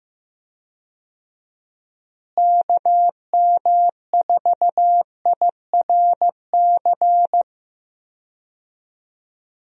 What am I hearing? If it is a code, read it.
Morse code, heard as a keyed tone, about 15 wpm, 700 Hz. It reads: KM4IRC
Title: USA Call Signs - 6 Characters - Repeated 3x - 15wpm